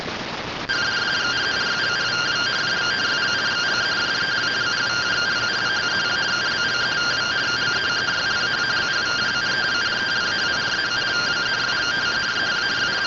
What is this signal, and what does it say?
Two-tone teleprinter signal, about 170 Hz shift.
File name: findings / CORONAL fr69 nb202 3mm.gen-41 rtty rx